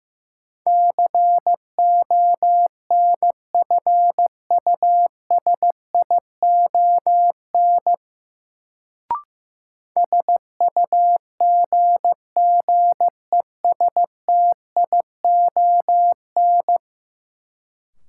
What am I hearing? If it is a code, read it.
Morse code, heard as a keyed tone, about 16 wpm, 700 Hz. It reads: CONFUSION SUGGESTION
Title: Sentences from Top 1600 Words - New Word Drill - 15wpm